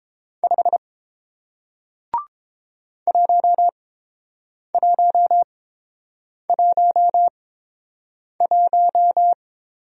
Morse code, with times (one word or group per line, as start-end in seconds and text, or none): 0.4-0.8 5
3.1-9.3 1111